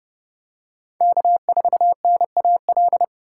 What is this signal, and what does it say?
Morse code, heard as a keyed tone, about 29 wpm, 700 Hz. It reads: K4NAL